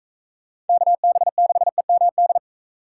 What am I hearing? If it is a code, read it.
Morse code, heard as a keyed tone, about 43 wpm, 700 Hz. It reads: KB6EMD